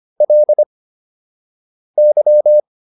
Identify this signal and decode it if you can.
Morse code, heard as a keyed tone, about 25 wpm, spaced as for 6 wpm, 600 Hz. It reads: LY